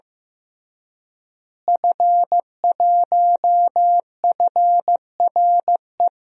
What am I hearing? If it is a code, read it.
Morse code, heard as a keyed tone, about 15 wpm, 700 Hz. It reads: F1FRE
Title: DXCC-100 Call Signs - 5 Characters - Repeated 3x - 15wpm